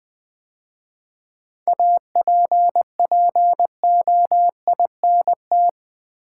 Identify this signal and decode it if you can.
Morse code, heard as a keyed tone, about 20 wpm, 700 Hz. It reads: APPOINT
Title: 7 Letter Words - First 3 Letters Given - 20wpm